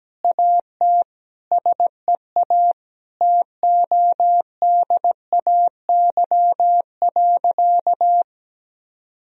Morse code, heard as a keyed tone, about 17 wpm, 700 Hz. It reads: AT SEA TODAY.
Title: Sentences from Top 1000 Words - Review Part 2 - 17wpm